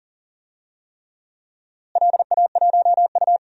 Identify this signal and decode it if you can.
Morse code, heard as a keyed tone, about 40 wpm, 700 Hz. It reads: LA1U
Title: DXCC-100 Call Signs - 4 Characters - Repeated 3x - 40wpm